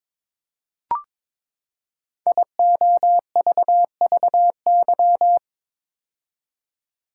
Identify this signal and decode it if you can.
Morse code, heard as a keyed tone, about 23 wpm, 700 Hz. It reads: IOVVY
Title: Code Groups - 5 Letter:Numbers - 22wpm